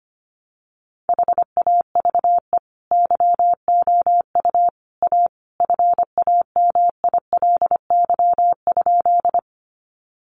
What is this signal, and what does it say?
Morse code, heard as a keyed tone, about 25 wpm, 700 Hz. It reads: HAVE YOU A FAMILY?
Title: Sentences from Top 400 Words 25wpm